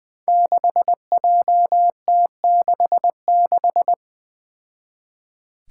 Morse code, heard as a keyed tone, about 20 wpm, 700 Hz. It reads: 6JT66
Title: Five Challenging Letter-Number Combinations 20wpm